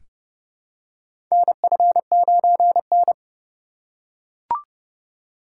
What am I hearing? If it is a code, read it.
Morse code, heard as a keyed tone, about 30 wpm, 700 Hz. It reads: NF9N
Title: USA Call Signs - 4 Characters - 4x Character Spacing - 30wpm